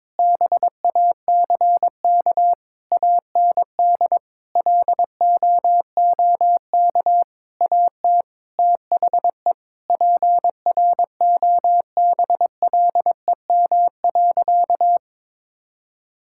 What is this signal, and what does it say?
Morse code, heard as a keyed tone, about 22 wpm, 700 Hz. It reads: BACK AND LOOK AT THE PROBLEM.
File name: Sentences with 7-Word Phrases - 22wpm